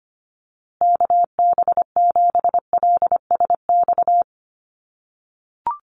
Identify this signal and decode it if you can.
Morse code, heard as a keyed tone, about 25 wpm, 700 Hz. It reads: KB7LSX
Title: USA Call Signs Repeated 2x 25wpm